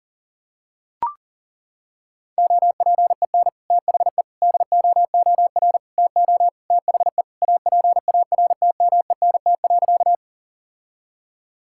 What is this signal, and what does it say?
Morse code, heard as a keyed tone, about 37 wpm, 700 Hz. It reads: OPEN THE DOOR TO THE APARTMENT.
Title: Sentences from Top 1000 Words - Review Part 6 - 40wpm